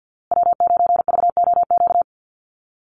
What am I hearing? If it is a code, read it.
Morse code, heard as a keyed tone, about 50 wpm, 700 Hz. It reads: W9VOQ
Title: USA Call Signs Repeated 3x 50wpm